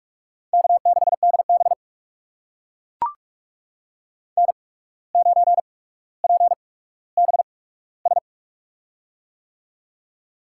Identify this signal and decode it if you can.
Morse code, heard as a keyed tone, about 44 wpm, spaced as for 18 wpm, 700 Hz. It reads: K6DB N9PBS
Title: USA Call Signs - 8x Character Spacing - 45wpm